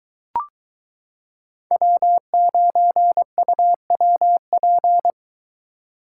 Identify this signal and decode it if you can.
Morse code, heard as a keyed tone, about 22 wpm, 700 Hz. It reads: W9UWP